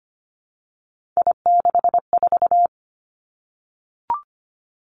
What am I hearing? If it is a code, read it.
Morse code, heard as a keyed tone, about 25 wpm, 700 Hz. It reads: I64
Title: Three Challenging Letter-Number Combinations 25wpm